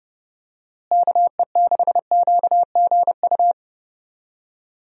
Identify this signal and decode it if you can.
Morse code, heard as a keyed tone, about 30 wpm, 700 Hz. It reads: KE6QGU
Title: USA Call Signs - 6 Characters - Repeated 3x - 30wpm